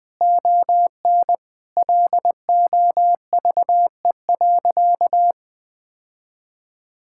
Morse code, heard as a keyed tone, about 19 wpm, 700 Hz. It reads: ON LOVE.
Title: Sentences from Top 1000 Words - Review Part 6 - 20wpm